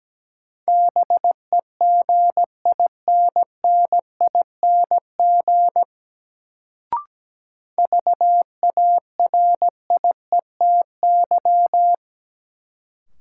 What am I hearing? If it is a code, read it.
Morse code, heard as a keyed tone, about 17 wpm, 700 Hz. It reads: BEGINNING VARIETY
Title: Sets of 3 Words - New Word Drill - 17wpm